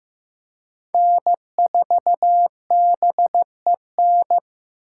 Morse code, heard as a keyed tone, about 15 wpm, 700 Hz. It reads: N4BEN
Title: USA Call Signs - 5 Characters - 4x Character Spacing - 15wpm